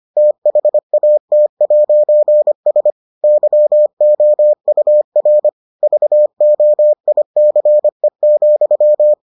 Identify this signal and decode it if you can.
Morse code, heard as a keyed tone, about 25 wpm, 600 Hz. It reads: THAT'S YOUR VOICE,